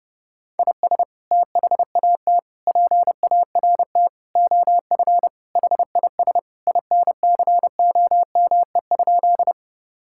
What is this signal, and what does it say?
Morse code, heard as a keyed tone, about 30 wpm, 700 Hz. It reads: IS THAT PART OF HIS INCOME?
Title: Sentences from Top 1000 Words - Review Part 5 - 30wpm